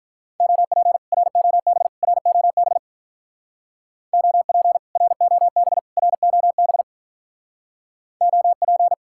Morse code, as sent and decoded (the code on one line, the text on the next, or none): --- .--. / .-. --- -... / .-. --- -... / --- .--. / .-. --- -... / .-. --- -... / --- .--.
OP ROB ROB OP ROB ROB OP